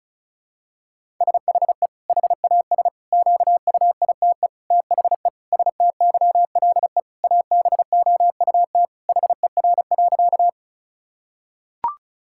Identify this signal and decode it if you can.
Morse code, heard as a keyed tone, about 36 wpm, 700 Hz. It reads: SHE HAS QUITE THE STYLE ABOUT HER.
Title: Sentences from Top 1000 Words - Review Part 5 - 35wpm